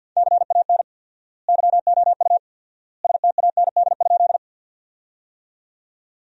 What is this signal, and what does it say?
Morse code, heard as a keyed tone, about 52 wpm, 700 Hz. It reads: CAN YOU STAND?
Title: 4x Word-Spacing - Sentences from Top 300 Words - 50wpm